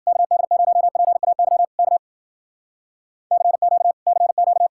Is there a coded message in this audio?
Yes: Morse code from 0.1 to 4.7 s, heard as a keyed tone, about 52 wpm, 700 Hz. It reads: KD0PAY K CQ CQ